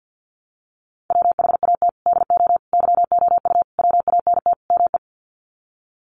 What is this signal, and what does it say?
Morse code, heard as a keyed tone, about 48 wpm, 700 Hz. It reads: WHAT DO YOU WANT ME
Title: Sets of 5 Words Encore2 50wpm